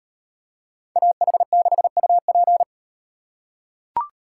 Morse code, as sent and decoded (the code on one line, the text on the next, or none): .- .... -.... ..- .--.
AH6UP